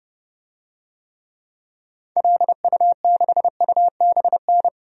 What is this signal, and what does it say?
Morse code, heard as a keyed tone, about 30 wpm, 700 Hz. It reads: LU6UBN
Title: DXCC-100 Call Signs - 6 Characters - 30wpm